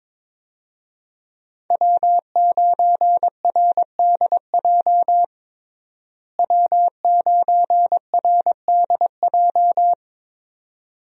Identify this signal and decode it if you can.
Morse code, heard as a keyed tone, about 21 wpm, 700 Hz. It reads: W9RDJ W9RDJ
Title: USA Call Signs - 5 Characters - Repeated 2x - 22wpm